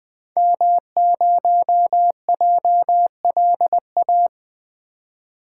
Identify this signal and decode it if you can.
Morse code, heard as a keyed tone, about 21 wpm, 700 Hz. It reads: M0JLA
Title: 2023 SOTA Activator Call Signs - 20wpm